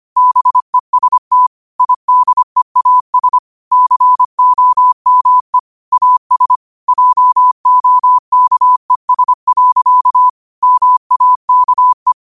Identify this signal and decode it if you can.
Morse code, heard as a keyed tone, about 25 wpm, 1000 Hz. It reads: BEST IDEAS COME AS JOKES. MAKE
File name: 25wpm